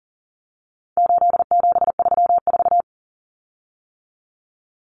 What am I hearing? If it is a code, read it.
Morse code, heard as a keyed tone, about 40 wpm, 700 Hz. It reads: 8734
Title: Common Cut Numbers - 4-Digit 40wpm